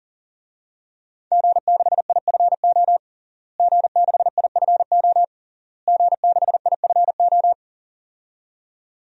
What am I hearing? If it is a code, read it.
Morse code, heard as a keyed tone, about 40 wpm, 700 Hz. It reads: G6IFO G6IFO G6IFO
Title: DXCC-100 Call Signs - 5 Characters - Repeated 3x - 40wpm